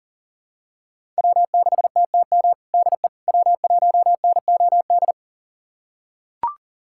Morse code, heard as a keyed tone, about 37 wpm, 700 Hz. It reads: W6TTM DE W1NOD
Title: USA Callsign DE Callsign - 5 Characters - 40wpm